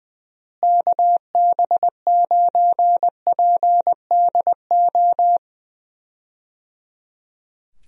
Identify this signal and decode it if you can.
Morse code, heard as a keyed tone, about 20 wpm, 700 Hz. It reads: KB9PDO
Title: USA Call Signs - 6 Characters - Repeated 3x - 20wpm